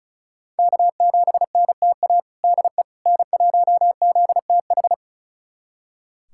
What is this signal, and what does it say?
Morse code, heard as a keyed tone, about 34 wpm, 700 Hz. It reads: K7NTA DE N1ZTH
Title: USA Callsign DE Callsign - 5 Characters - 35wpm